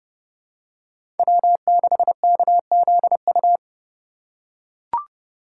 Morse code, heard as a keyed tone, about 32 wpm, 700 Hz. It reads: W6KZU
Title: USA Call Signs - 5 Characters - 3x Character Spacing - 30wpm